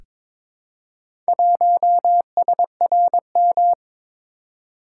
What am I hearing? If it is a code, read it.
Morse code, heard as a keyed tone, about 23 wpm, 700 Hz. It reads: 1SRM
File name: practice / 22wpm / Four Letter-Number Combinations - 22wpm